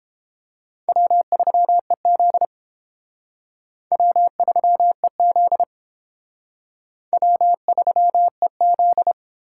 Morse code, heard as a keyed tone, about 29 wpm, 700 Hz. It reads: W3EZ W3EZ W3EZ